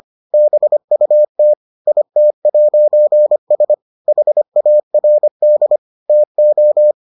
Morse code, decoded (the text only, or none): BUT IT'S HARD TO